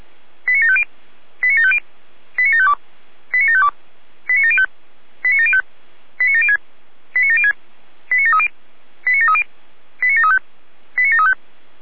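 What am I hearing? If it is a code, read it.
Five-tone selective call (ZVEI), heard as quick runs of five tones, tones of about 80 ms. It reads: 89750 89750 89752 89752 89705 89705 89706 89706 89730 89730 89735 89735
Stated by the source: ZVEI five-tone selective call 89465 received by SDR.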